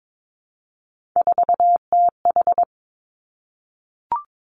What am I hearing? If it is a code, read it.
Morse code, heard as a keyed tone, about 22 wpm, 700 Hz. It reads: 4TH
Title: Three Challenging Letter-Number Combinations 22wpm